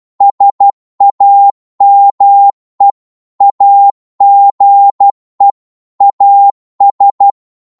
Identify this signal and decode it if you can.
Morse code, heard as a keyed tone, about 12 wpm, 800 Hz. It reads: SAME AGE AS